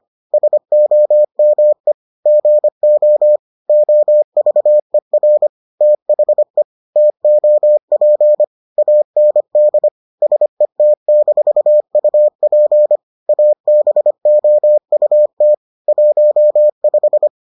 Morse code, heard as a keyed tone, about 25 wpm, 600 Hz. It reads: SOME GO OVER THE TOP AND SET-UP ABOUT 15